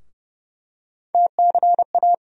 Morse code, heard as a keyed tone, about 30 wpm, 700 Hz. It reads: TCA
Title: USA Call Sign Postfixes - 3 Letters 30wpm